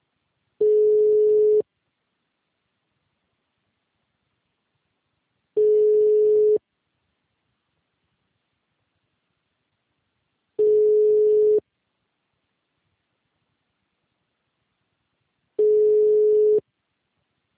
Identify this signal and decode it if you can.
Ringback tone (Europe, 425 Hz).